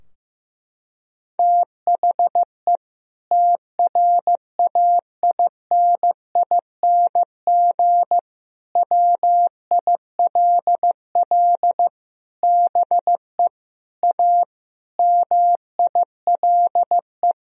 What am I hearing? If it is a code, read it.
Morse code, heard as a keyed tone, about 16 wpm, 700 Hz. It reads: THE TRAINING WILL BE A MILE
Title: Sentences from Top 1000 Words - Review Part 4 - 15wpm